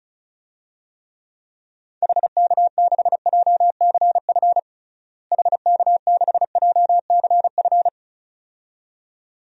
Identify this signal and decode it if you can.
Morse code, heard as a keyed tone, about 36 wpm, 700 Hz. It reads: HK6JCF HK6JCF
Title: DXCC-100 Call Signs - 6 Characters - Repeated 2x - 35wpm